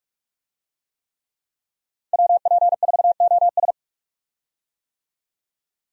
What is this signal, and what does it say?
Morse code, heard as a keyed tone, about 43 wpm, 700 Hz. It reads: WP4OS
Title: USA Call Signs - 5 Characters - 45wpm